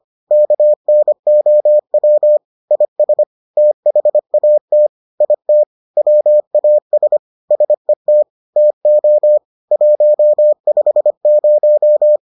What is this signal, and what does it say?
Morse code, heard as a keyed tone, about 25 wpm, 600 Hz. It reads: KNOW IS THAT IT WAS SET TO 150